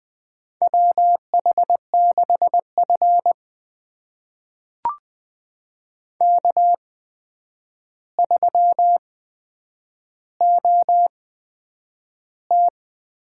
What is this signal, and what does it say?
Morse code, heard as a keyed tone, about 20 wpm, spaced as for 9 wpm, 700 Hz. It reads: WH6F K3OT
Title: USA Call Signs - 8x Character Spacing - 20wpm